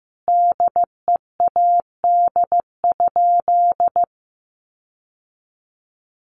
Morse code, heard as a keyed tone, about 15 wpm, 700 Hz. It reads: DEAD?
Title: Sentences from Top 1500 Words 15wpm